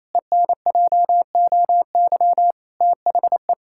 Morse code, heard as a keyed tone, about 28 wpm, 700 Hz. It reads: ENJOY THE
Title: Sentences from Top 1000 Words - Review Part 5 - 28wpm